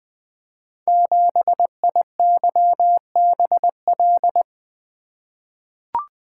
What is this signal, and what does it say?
Morse code, heard as a keyed tone, about 20 wpm, 700 Hz. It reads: 7IYBL
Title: Code Groups - 5 Letter:Numbers - 20wpm